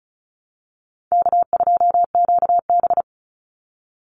Morse code, heard as a keyed tone, about 34 wpm, 700 Hz. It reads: K2QB
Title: Easy Call Signs Repeated 3x - 35wpm